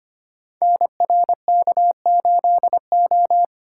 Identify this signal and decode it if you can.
Morse code, heard as a keyed tone, about 25 wpm, 700 Hz. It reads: NRK8O
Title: Code Groups - 5 Letter:Numbers - 25wpm